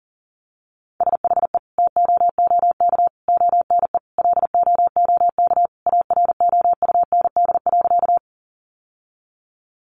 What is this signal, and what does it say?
Morse code, heard as a keyed tone, about 40 wpm, 700 Hz. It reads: SHE TOOK ONE LOOK AROUND.
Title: Sentences from Top 600 Words 40wpm